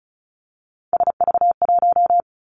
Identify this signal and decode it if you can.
Morse code, heard as a keyed tone, about 34 wpm, 700 Hz. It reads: SV1
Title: Callsign Prefixes for Top 50 DXCC - 35wpm